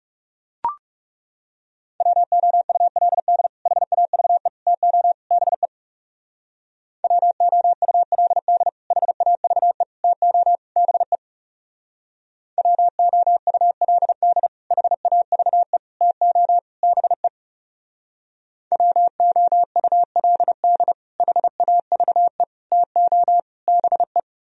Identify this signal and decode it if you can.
Morse code, heard as a keyed tone, about 37 wpm, 700 Hz. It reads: WOULD HAVE TO BE WOULD HAVE TO BE WOULD HAVE TO BE WOULD HAVE TO BE
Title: Speed-Racing - Sets of 4 Words 30wpm